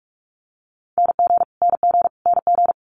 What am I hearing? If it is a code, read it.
Morse code, heard as a keyed tone, about 46 wpm, 700 Hz. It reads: NG NG NG
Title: Multi-letter Phonemes Repeated 3x 45wpm